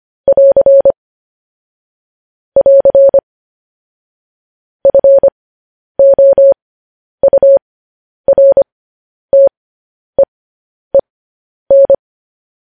Morse code, heard as a keyed tone, about 25 wpm, spaced as for 10 wpm, 550 Hz. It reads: <AR> <AR> FOURTEEN
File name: podcast / News headlines-1x-25x10